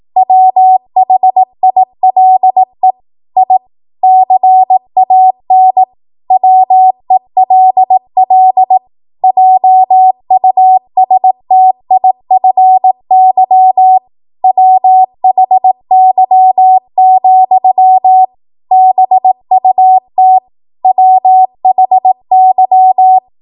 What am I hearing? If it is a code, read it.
Morse code, heard as a keyed tone, about 18 wpm, 750 Hz. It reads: WHILE I CAN WELL JUSTIFY WHY, BUT WHY
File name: Morse / 18WPM